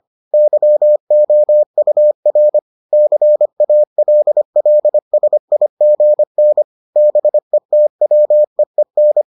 Morse code, heard as a keyed tone, about 25 wpm, 600 Hz. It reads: YOUR CALLSIGN BETWEEN